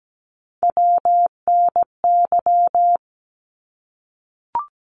Morse code, heard as a keyed tone, about 17 wpm, 700 Hz. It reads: WNY
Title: ARRL Sections 17wpm